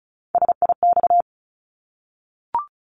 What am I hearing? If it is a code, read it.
Morse code, heard as a keyed tone, about 34 wpm, 700 Hz. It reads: SIX
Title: Top 500 Words 35wpm